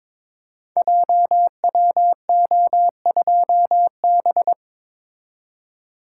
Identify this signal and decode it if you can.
Morse code, heard as a keyed tone, about 22 wpm, 700 Hz. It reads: JWO2B
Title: Code Groups - 5 Letter:Numbers - 22wpm